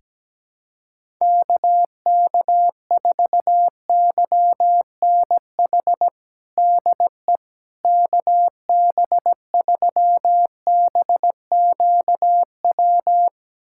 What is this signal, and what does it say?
Morse code, heard as a keyed tone, about 17 wpm, 700 Hz. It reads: KK4YNH DE KB3BQW
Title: USA Callsign DE Callsign - 6 Characters - 17wpm